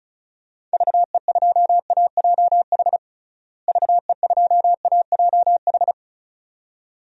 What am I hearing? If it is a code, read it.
Morse code, heard as a keyed tone, about 34 wpm, 700 Hz. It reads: VE2AJH VE2AJH
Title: DXCC-100 Call Signs - 6 Characters - Repeated 2x - 35wpm